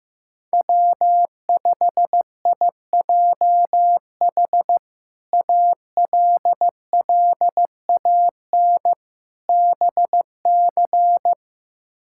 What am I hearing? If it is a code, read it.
Morse code, heard as a keyed tone, about 15 wpm, 700 Hz. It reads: W5IJH ALLAN BC
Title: K1USN SST QSO Practice - 15wpm